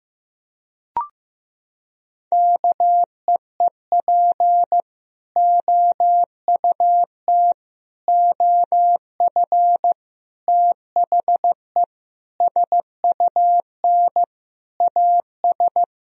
Morse code, heard as a keyed tone, about 16 wpm, 700 Hz. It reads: KEEP OUT OF THE SUN AS
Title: Sentences from Top 1000 Words - Review Part 1 - 15wpm